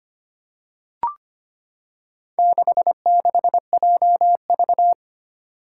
Morse code, heard as a keyed tone, about 25 wpm, 700 Hz. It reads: 66JV
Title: Four Challenging Letter-Number Combinations 25wpm